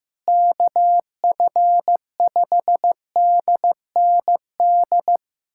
Morse code, heard as a keyed tone, about 15 wpm, 700 Hz. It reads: KF5DND
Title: USA Callsign DE Callsign 15wpm